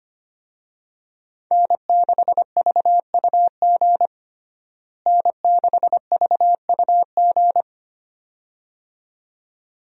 Morse code, heard as a keyed tone, about 25 wpm, 700 Hz. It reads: N6VUG N6VUG
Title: USA Call Signs - 5 Characters - Repeated 2x - 25wpm